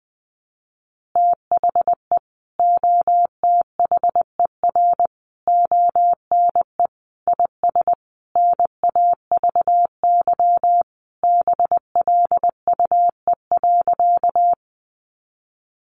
Morse code, heard as a keyed tone, about 20 wpm, 700 Hz. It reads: THE OTHER ONE IS NAVY BLUE.
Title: Additional Easy Phrases 20wpm